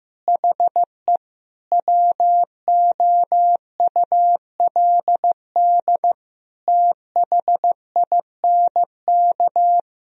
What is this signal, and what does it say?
Morse code, heard as a keyed tone, about 15 wpm, 700 Hz. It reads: HE WOULD THINK
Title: Sentences from Top 1000 Words - Review Part 2 - 15wpm